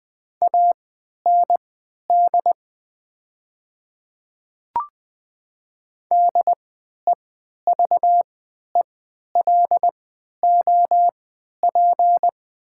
Morse code, heard as a keyed tone, about 20 wpm, spaced as for 11 wpm, 700 Hz. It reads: AND DEVELOP
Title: 3x Character Spacing - Top 500 Words - 20wpm